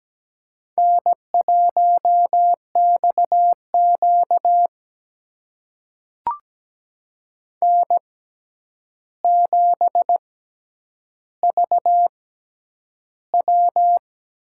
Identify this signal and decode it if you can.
Morse code, heard as a keyed tone, about 17 wpm, spaced as for 9 wpm, 700 Hz. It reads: N1XQ N7VW
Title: USA Call Signs - 6x Character Spacing - 17wpm